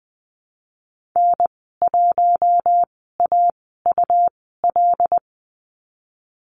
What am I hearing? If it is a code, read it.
Morse code, heard as a keyed tone, about 20 wpm, spaced as for 15 wpm, 700 Hz. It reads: N1AUL
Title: 147 Lesson 24g - Letter y - Call Signs 20wpm